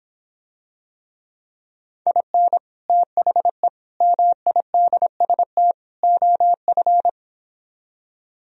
Morse code, heard as a keyed tone, about 26 wpm, 700 Hz. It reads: IN THE MIDST OF